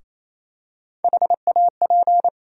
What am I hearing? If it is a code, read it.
Morse code, heard as a keyed tone, about 28 wpm, 700 Hz. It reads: HAP
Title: USA Call Sign Postfixes - 3 Letters 28wpm